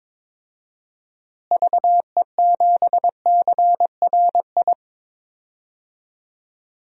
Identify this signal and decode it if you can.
Morse code, heard as a keyed tone, about 22 wpm, 700 Hz. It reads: VE7CRI
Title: DXCC-100 Call Signs - 6 Characters - Repeated 3x - 22wpm